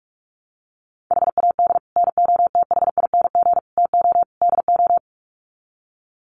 Morse code, heard as a keyed tone, about 44 wpm, 700 Hz. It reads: HAD NOTHING TO DO
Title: Sets of 4 Words 45wpm